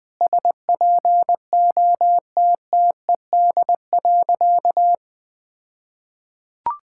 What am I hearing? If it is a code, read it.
Morse code, heard as a keyed tone, about 20 wpm, 700 Hz. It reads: SPOTTED.
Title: All-Age Appropriate Jokes - 20wpm